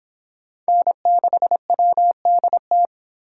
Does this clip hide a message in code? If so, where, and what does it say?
Yes: Morse code from 0.7 to 2.9 s, heard as a keyed tone, about 25 wpm, 700 Hz. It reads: N6WDT